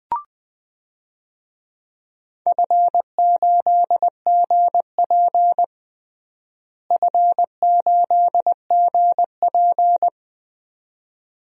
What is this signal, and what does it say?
Morse code, heard as a keyed tone, about 20 wpm, 700 Hz. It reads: F8GP F8GP
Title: DXCC-100 Call Signs - 4 Characters - Repeated 2x - 20wpm